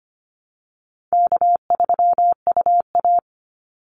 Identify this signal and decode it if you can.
Morse code, heard as a keyed tone, about 25 wpm, 700 Hz. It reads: K3UA
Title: Call Signs Repeated 3x - 25wpm